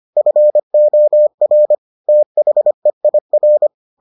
Morse code, heard as a keyed tone, about 25 wpm, 600 Hz. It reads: FOR THEIR